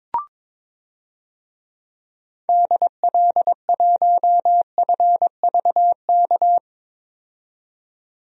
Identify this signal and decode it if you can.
Morse code, heard as a keyed tone, about 22 wpm, 700 Hz. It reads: DL1FVK
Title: DXCC-100 Call Signs - 6 Characters - 22wpm